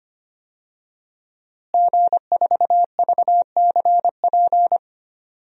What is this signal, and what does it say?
Morse code, heard as a keyed tone, about 25 wpm, 700 Hz. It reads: G4VCP
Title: DXCC-100 Call Signs - 5 Characters - Repeated 2x - 25wpm